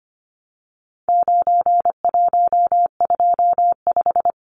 Morse code, heard as a keyed tone, about 25 wpm, 700 Hz. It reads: 9125
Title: Numbers - 4-Digit 25wpm